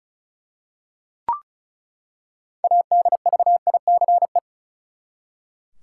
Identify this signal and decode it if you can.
Morse code, heard as a keyed tone, about 36 wpm, 700 Hz. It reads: ADVICE
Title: Six Letter Words - 35wpm